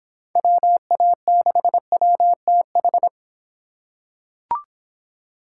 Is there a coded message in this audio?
Yes: Morse code from 0.4 to 3.1 s, heard as a keyed tone, about 26 wpm, 700 Hz. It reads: WA6WTH